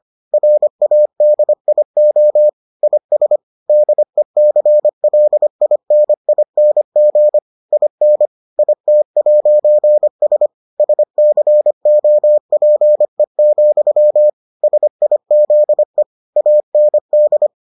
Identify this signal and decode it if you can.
Morse code, heard as a keyed tone, about 25 wpm, 600 Hz. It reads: RADIO IS DECLINING IN IT'S SCOPE, SIZE AND